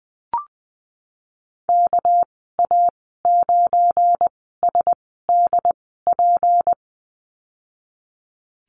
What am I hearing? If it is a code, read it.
Morse code, heard as a keyed tone, about 20 wpm, spaced as for 15 wpm, 700 Hz. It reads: KA9SDP
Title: 219 Lesson 35g - Number 0 - Call Signs 20wpm